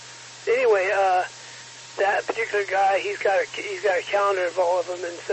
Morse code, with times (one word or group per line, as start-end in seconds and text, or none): none